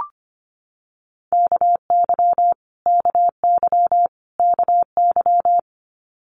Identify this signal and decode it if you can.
Morse code, heard as a keyed tone, about 25 wpm, 700 Hz. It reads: KY KY KY
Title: Multi-letter Phonemes Repeated 3x 25wpm